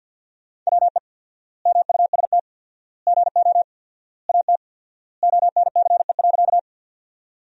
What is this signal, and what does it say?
Morse code, heard as a keyed tone, about 50 wpm, 700 Hz. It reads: WE MUST GO AT ONCE.
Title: 4x Word-Spacing - Sentences from Top 300 Words - 50wpm